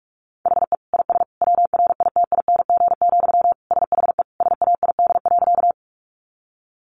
Morse code, heard as a keyed tone, about 44 wpm, 700 Hz. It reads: HE IS WRITING, SHE SAID.
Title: Sentences from Top 800 Words 45wpm